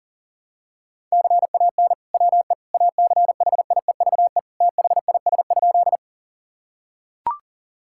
Morse code, heard as a keyed tone, about 40 wpm, 700 Hz. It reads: CAN WE ACHIEVE THIS?
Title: Sentences from Top 1000 Words - Review Part 6 - 40wpm